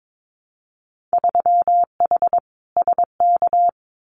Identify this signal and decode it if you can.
Morse code, heard as a keyed tone, about 22 wpm, 700 Hz. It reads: 3H SK
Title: Winter Field Day Exchanges 22wpm